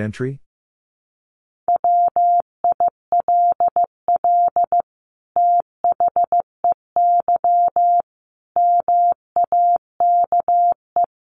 Morse code, heard as a keyed tone, about 14 wpm, 700 Hz. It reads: WILL THEY MAKE
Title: Sentences from Top 1500 Words 15wpm